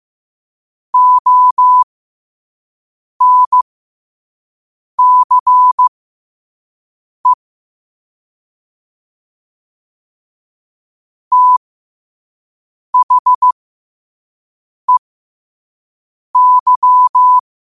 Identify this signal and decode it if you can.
Morse code, heard as a keyed tone, about 15 wpm, spaced as for 5 wpm, 1000 Hz. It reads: ONCE THEY
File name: 5wpm